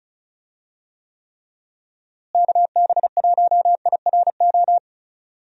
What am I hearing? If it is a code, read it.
Morse code, heard as a keyed tone, about 36 wpm, 700 Hz. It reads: KB1IRO
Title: USA Call Signs - 6 Characters - Repeated 3x - 35wpm